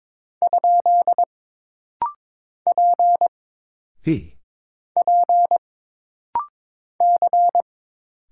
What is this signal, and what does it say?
Morse code, heard as a keyed tone, about 22 wpm, 700 Hz. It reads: ? P P C
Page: This rapid-fire timing is geared to push your ICR (Instant Character Recognition) skills.